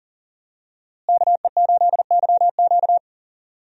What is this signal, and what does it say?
Morse code, heard as a keyed tone, about 39 wpm, 700 Hz. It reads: KE8YQ
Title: USA Call Signs - 5 Characters - Repeated 2x - 40wpm